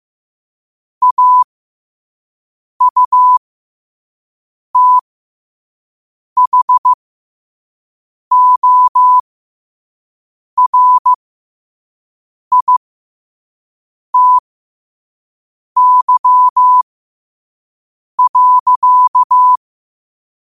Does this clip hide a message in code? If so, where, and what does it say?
Yes: Morse code from 1.0 to 19.6 s, heard as a keyed tone, about 15 wpm, spaced as for 5 wpm, 1000 Hz. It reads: AUTHORITY.